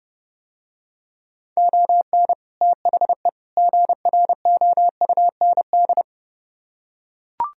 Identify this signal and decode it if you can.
Morse code, heard as a keyed tone, about 29 wpm, 700 Hz. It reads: ON THE GROUND